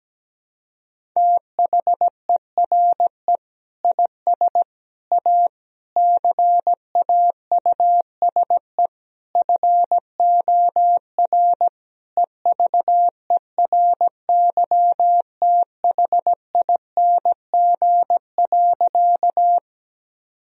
Morse code, heard as a keyed tone, about 17 wpm, 700 Hz. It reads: THERE IS A CAUSE FOR EVERYTHING.
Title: Sentences from Top 1000 Words - Review Part 1 - 17wpm